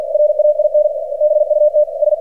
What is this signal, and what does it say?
Morse code, heard as a keyed tone, about 47 wpm, 600 Hz, with radio noise. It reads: RAIN BUT S
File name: archived / CRSnet